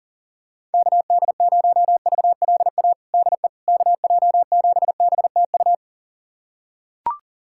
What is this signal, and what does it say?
Morse code, heard as a keyed tone, about 37 wpm, 700 Hz. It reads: KD0VLA DE KJ7BTU
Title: USA Callsign DE Callsign - 6 Characters - 40wpm